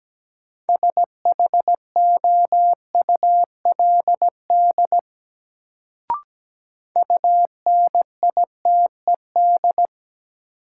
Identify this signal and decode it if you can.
Morse code, heard as a keyed tone, about 17 wpm, 700 Hz. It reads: SHOULD UNITED
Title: Sets of 2 Words - New Word Drill - 17wpm